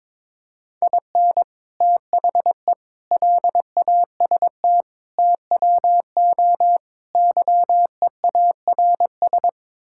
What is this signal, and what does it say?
Morse code, heard as a keyed tone, about 22 wpm, 700 Hz. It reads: IN THE LAST TWO YEARS
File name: Build A Phrase - 5-Word Phrases - 22wpm